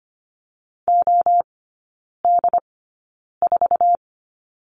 Morse code, heard as a keyed tone, about 25 wpm, spaced as for 9 wpm, 700 Hz. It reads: OD4
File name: Lesson 15e - Number 5 - Three Characters 25wpm